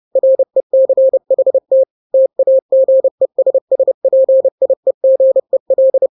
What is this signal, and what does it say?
Morse code, heard as a keyed tone, about 30 wpm, 530 Hz. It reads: RECHT TAGESSPIEGEL
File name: 29wpm